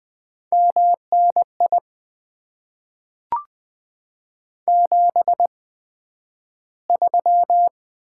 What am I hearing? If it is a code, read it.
Morse code, heard as a keyed tone, about 20 wpm, spaced as for 11 wpm, 700 Hz. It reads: MNI 73
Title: Top 90 QSO Elements - 8x Character Spacing - 20wpm